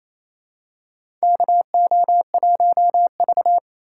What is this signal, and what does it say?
Morse code, heard as a keyed tone, about 28 wpm, 700 Hz. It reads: KO1V
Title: USA Call Signs - 4x Character Spacing - 28wpm